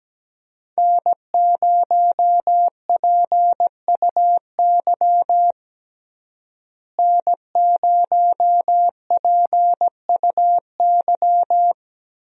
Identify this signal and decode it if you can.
Morse code, heard as a keyed tone, about 17 wpm, 700 Hz. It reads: N0PUY N0PUY
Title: USA Call Signs - 5 Characters - Repeated 2x - 17wpm